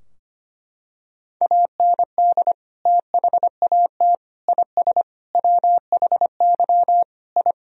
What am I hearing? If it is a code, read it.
Morse code, heard as a keyed tone, about 25 wpm, 700 Hz. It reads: AND THAT IS WHY I
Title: Sentences from Top 1000 Words - Review Part 3 - 25wpm